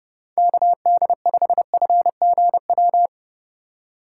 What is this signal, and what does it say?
Morse code, heard as a keyed tone, about 30 wpm, 700 Hz. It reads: KD5FGW
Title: USA Call Signs - 6 Characters - 4x Character Spacing - 30wpm